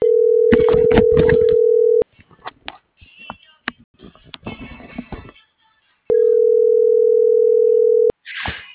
Ringback tone (North America), 440 + 480 Hz.